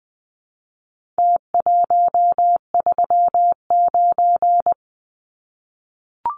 Morse code, heard as a keyed tone, about 20 wpm, 700 Hz. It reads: T139
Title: Numbers - Contest Style - 4-Digit 20wpm